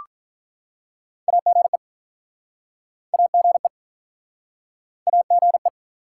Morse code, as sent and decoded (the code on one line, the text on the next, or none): .- --. / .- --. / .- --. .
AGE AGE AGE